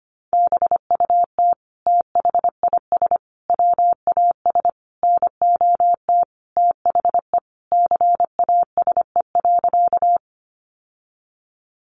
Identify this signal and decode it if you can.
Morse code, heard as a keyed tone, about 25 wpm, 700 Hz. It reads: BUT THIS WAS NOT THE CASE.
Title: Sentences from Top 700 Words 25wpm